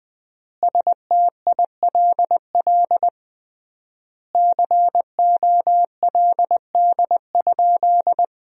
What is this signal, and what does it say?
Morse code, heard as a keyed tone, about 20 wpm, 700 Hz. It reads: STILL COLD?
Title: 3x Word-Spacing - Sentences from Top 400 Words - 20wpm